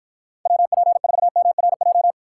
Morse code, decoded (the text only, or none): WP4MRJ